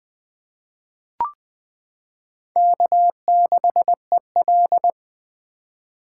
Morse code, heard as a keyed tone, about 20 wpm, 700 Hz. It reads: K6EL
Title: 2023 SOTA Chaser Call Signs - 20wpm